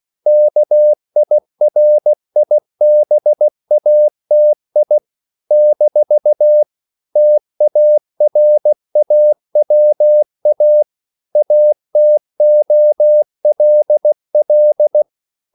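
Morse code, heard as a keyed tone, about 16 wpm, 600 Hz. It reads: KIRIBATI - TARAWA ATOLL
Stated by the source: CW practice of national capitals at 16 words per minute, from ON6ZQ.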